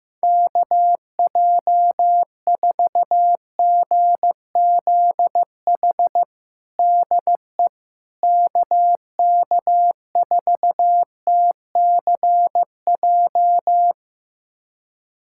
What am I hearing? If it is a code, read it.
Morse code, heard as a keyed tone, about 15 wpm, 700 Hz. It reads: KJ4GZH DE KK4TCJ
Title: USA Callsign DE Callsign - 6 Characters - 15wpm